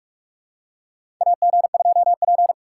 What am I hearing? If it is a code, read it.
Morse code, heard as a keyed tone, about 46 wpm, 700 Hz. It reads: AG2P
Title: USA Call Signs - 4 Characters - 4x Character Spacing - 45wpm